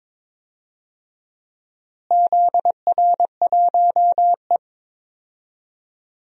Morse code, heard as a keyed tone, about 22 wpm, 700 Hz. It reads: ZR1E
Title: DXCC-100 Call Signs - 4 Characters - Repeated 2x - 22wpm